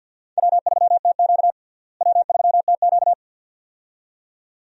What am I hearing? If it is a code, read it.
Morse code, heard as a keyed tone, about 47 wpm, 700 Hz. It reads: W3TQ W3TQ
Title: USA Call Signs - 4 Characters - Repeated 2x - 50wpm